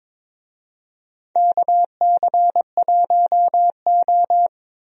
Morse code, heard as a keyed tone, about 22 wpm, 700 Hz. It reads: KC1O
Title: USA Call Signs - 4 Characters - Repeated 2x - 22wpm